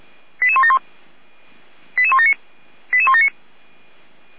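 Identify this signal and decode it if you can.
Five-tone selective call (ZVEI), heard as quick runs of five tones, tones of about 80 ms. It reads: 88171 88179 88179
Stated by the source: Five-tone ZVEI selective call received on SDR.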